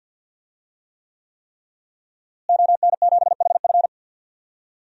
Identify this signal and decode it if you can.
Morse code, heard as a keyed tone, about 52 wpm, 700 Hz. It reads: ON7HF
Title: DXCC-100 Call Signs - 5 Characters - Repeated 3x - 50wpm